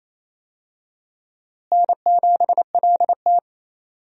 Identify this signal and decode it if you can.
Morse code, heard as a keyed tone, about 28 wpm, 700 Hz. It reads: N7LT
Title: USA Call Signs - 4 Characters - Repeated 2x - 28wpm